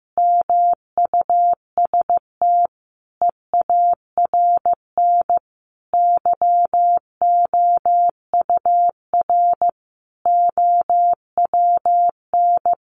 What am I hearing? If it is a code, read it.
Morse code, heard as a keyed tone, about 15 wpm, 700 Hz. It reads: MUST EARN YOUR OWN
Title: Sentences from Top 1500 Words 15wpm